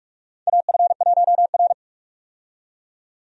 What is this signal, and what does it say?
Morse code, heard as a keyed tone, about 43 wpm, 700 Hz. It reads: AF1R
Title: USA Call Signs - 4 Characters - 45wpm